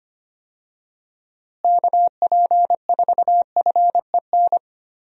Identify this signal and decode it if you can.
Morse code, heard as a keyed tone, about 25 wpm, 700 Hz. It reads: KP4FEN
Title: DXCC-100 Call Signs - 6 Characters - Repeated 2x - 25wpm